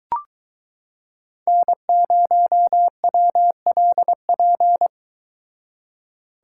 Morse code, heard as a keyed tone, about 22 wpm, 700 Hz. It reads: N0WLP